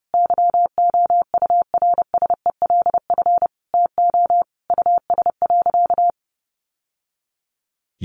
Morse code, heard as a keyed tone, about 30 wpm, 700 Hz. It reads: YOURSELF TO US.
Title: Sentences from Top 600 Words 30wpm